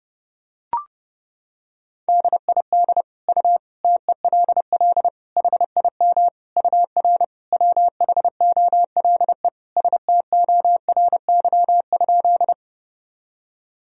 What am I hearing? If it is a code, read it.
Morse code, heard as a keyed tone, about 29 wpm, 700 Hz. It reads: DID U TELL HIM UR WHOLE STORY?
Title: Sentences from Top 400 Words and Abbreviations 30wpm